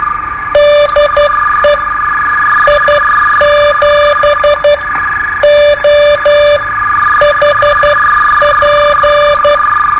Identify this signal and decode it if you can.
Morse code, heard as a keyed tone, about 12 wpm, spaced as for 9 wpm, 590 Hz, with noise. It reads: DE I7OHP
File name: qso-01